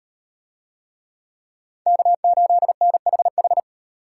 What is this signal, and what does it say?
Morse code, heard as a keyed tone, about 38 wpm, 700 Hz. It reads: K8NHH